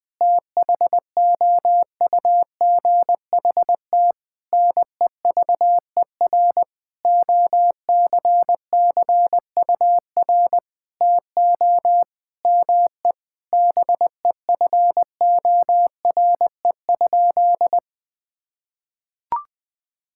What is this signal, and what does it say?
Morse code, heard as a keyed tone, about 20 wpm, 700 Hz. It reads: THOUGHT NEVER OCCUR TO ME BEFORE?
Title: Sentences from Top 1000 Words - Review Part 6 - 20wpm